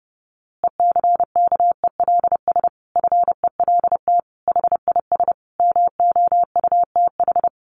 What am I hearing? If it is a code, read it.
Morse code, heard as a keyed tone, about 30 wpm, 700 Hz. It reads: ECKELS FELT HIS MOUTH
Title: A Sound Of Thunder 30wpm